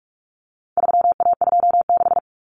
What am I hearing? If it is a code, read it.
Morse code, heard as a keyed tone, about 44 wpm, 700 Hz. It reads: 3A26
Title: Common Cut Numbers - 4-Digit 45wpm